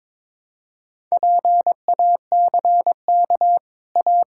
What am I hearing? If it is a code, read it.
Morse code, heard as a keyed tone, about 22 wpm, 700 Hz. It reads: PACK A
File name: practice / 22wpm / Easy Sentences with Hints - 22wpm